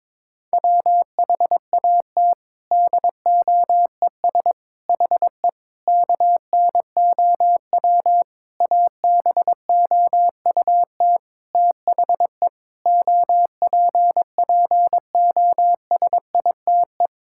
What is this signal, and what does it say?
Morse code, heard as a keyed tone, about 22 wpm, 700 Hz. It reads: WHAT DOES HE KNOW ABOUT THE OPPOSITE
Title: Sentences from Top 1600 Words 22wpm